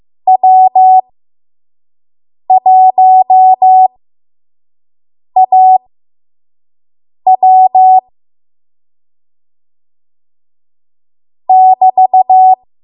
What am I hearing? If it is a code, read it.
Morse code, heard as a keyed tone, about 15 wpm, spaced as for 5 wpm, 750 Hz. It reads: W1AW <BT>
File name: Morse / 05WPM